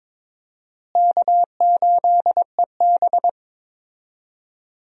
Morse code, heard as a keyed tone, about 22 wpm, 700 Hz. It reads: K8EB
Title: USA Call Signs - 4 Characters - Repeated 2x - 22wpm